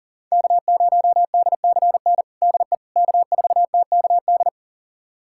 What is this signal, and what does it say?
Morse code, heard as a keyed tone, about 40 wpm, 700 Hz. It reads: K0DCN DE K4TKD
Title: USA Callsign DE Callsign - 5 Characters - 40wpm